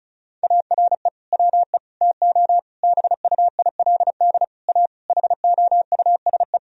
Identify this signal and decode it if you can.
Morse code, heard as a keyed tone, about 36 wpm, 700 Hz. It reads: ARE WE TO BUILD A HOUSE
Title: Sentences from Top 1000 Words - Review Part 1 - 35wpm